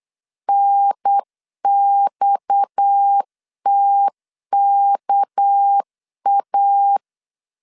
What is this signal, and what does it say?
Morse code, heard as a keyed tone, about 8 wpm, 800 Hz. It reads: NXTKA
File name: LEZIONI / 7 GF9Z / 7 GF 8wpm 3108